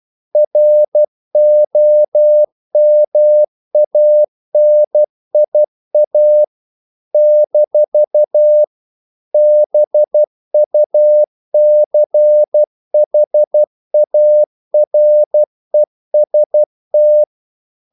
Morse code, heard as a keyed tone, about 12 wpm, 600 Hz. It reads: ROMANIA - BUCHAREST
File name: CW / Practice / NationalCapitals12